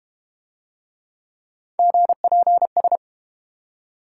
Morse code, heard as a keyed tone, about 31 wpm, 700 Hz. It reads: GPS
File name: Easy Words - Speed-Racing with Hints - 25wpm